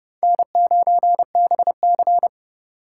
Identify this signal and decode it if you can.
Morse code, heard as a keyed tone, about 30 wpm, 700 Hz. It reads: N9BC